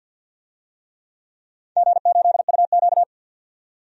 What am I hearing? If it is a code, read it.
Morse code, heard as a keyed tone, about 51 wpm, 700 Hz. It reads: G8UQ
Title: DXCC-100 Call Signs - 4 Characters - 50wpm